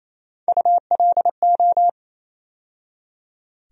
Morse code, heard as a keyed tone, about 28 wpm, 700 Hz. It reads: ULO
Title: USA Call Sign Postfixes - 3 Letters 28wpm